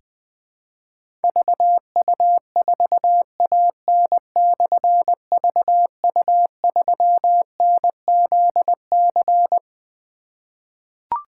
Morse code, heard as a keyed tone, about 20 wpm, 700 Hz. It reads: VU4AN/VU3NZC
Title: Difficult Call Signs 20wpm